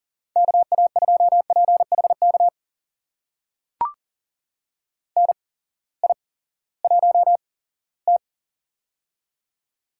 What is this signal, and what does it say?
Morse code, heard as a keyed tone, about 40 wpm, spaced as for 20 wpm, 700 Hz. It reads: KA2PHK NI1T
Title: USA Call Signs - 8x Character Spacing - 40wpm